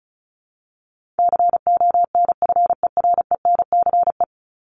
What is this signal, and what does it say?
Morse code, heard as a keyed tone, about 34 wpm, 700 Hz. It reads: CONFERENCE
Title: Top 1751-2000 Words 35wpm